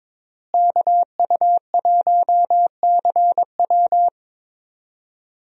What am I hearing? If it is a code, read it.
Morse code, heard as a keyed tone, about 21 wpm, 700 Hz. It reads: KU1CW
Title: USA Call Signs - 5 Characters - Repeated 3x - 22wpm